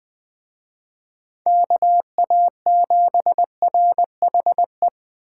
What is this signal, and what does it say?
Morse code, heard as a keyed tone, about 20 wpm, 700 Hz. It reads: KA7RHE
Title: Speed Racing - USA 2x3 Call Signs 17wpm